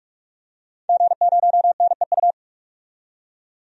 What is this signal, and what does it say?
Morse code, heard as a keyed tone, about 43 wpm, 700 Hz. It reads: G0NEU
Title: DXCC-100 Call Signs - 5 Characters - 45wpm